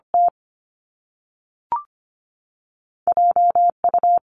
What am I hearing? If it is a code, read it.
Morse code, heard as a keyed tone, about 25 wpm, 700 Hz. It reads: T J U
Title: Top Two Letter Combinations in Words 25wpm